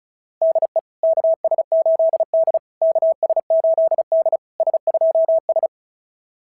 Morse code, heard as a keyed tone, about 36 wpm, 660 Hz. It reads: DE KS8D KS8D S2S
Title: SOTA QSO Practice 35wpm